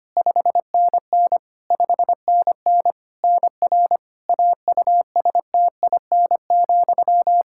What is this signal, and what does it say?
Morse code, heard as a keyed tone, about 25 wpm, 700 Hz. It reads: 5NN 5NN NR AUSTIN,
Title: Practice QSO - Exchange 1 - Beginning - 25wpm